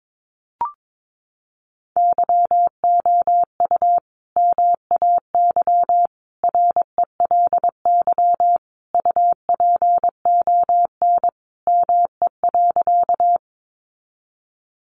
Morse code, heard as a keyed tone, about 22 wpm, 700 Hz. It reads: YOU MAY RELY UPON ME.
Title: Sentences from Top 1200 Words 22wpm